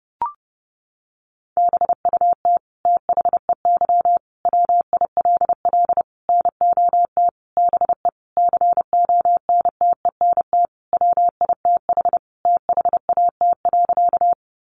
Morse code, heard as a keyed tone, about 28 wpm, 700 Hz. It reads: BUT THEY WILL NOT BE CONTENT WITH THAT.
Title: Sentences from Top 900 Words 30wpm